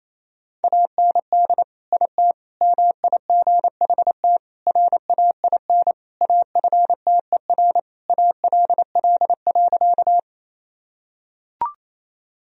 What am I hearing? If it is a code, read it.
Morse code, heard as a keyed tone, about 28 wpm, 700 Hz. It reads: AND IT MIGHT RAIN AFTER ALL.
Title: Sentences from Top 1000 Words - Review Part 1 - 28wpm